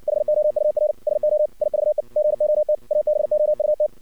Morse code, heard as a keyed tone, about 59 wpm, 610 Hz. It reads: HOUR SO IVE GOT ABOUT
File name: archived / CRSnet